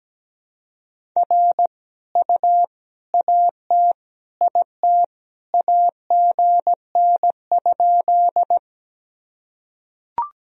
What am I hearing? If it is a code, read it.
Morse code, heard as a keyed tone, about 17 wpm, 700 Hz. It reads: R U AT IT AGN?
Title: Sentences from Top 200 Words and Abbreviations 17wpm